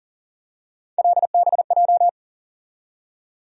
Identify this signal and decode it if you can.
Morse code, heard as a keyed tone, about 40 wpm, 700 Hz. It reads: LBJ
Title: USA Call Sign Postfixes - 3 Letters 40wpm